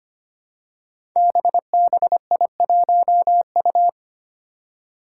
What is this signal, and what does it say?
Morse code, heard as a keyed tone, about 25 wpm, 700 Hz. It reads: BBI1U
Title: Five Challenging Letter-Number Combinations 25wpm